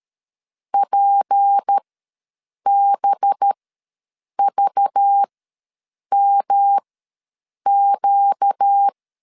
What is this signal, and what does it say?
Morse code, heard as a keyed tone, about 13 wpm, spaced as for 7 wpm, 800 Hz. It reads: PBVMQ
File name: LEZIONI / 6 PKX0 / 6 P ws 12wpm 3117